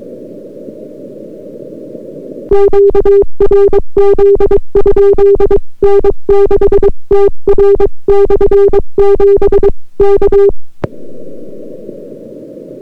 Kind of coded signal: Morse code